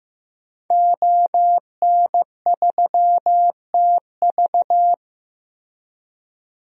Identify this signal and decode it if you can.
Morse code, heard as a keyed tone, about 16 wpm, 700 Hz. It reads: ON3TV
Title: DXCC-100 Call Signs - 5 Characters - Repeated 3x - 15wpm